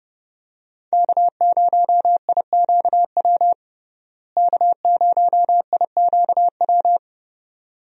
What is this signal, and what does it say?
Morse code, heard as a keyed tone, about 29 wpm, 700 Hz. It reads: K0IQW K0IQW
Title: USA Call Signs - 5 Characters - Repeated 2x - 30wpm